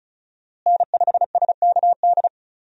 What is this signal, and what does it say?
Morse code, heard as a keyed tone, about 34 wpm, 700 Hz. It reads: N5SKD
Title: USA Call Signs - 5 Characters - Repeated 2x - 35wpm